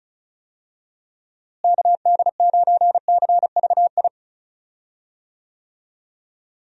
Morse code, heard as a keyed tone, about 36 wpm, 700 Hz. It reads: KD9CVI
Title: USA Call Signs - 6 Characters - Repeated 3x - 35wpm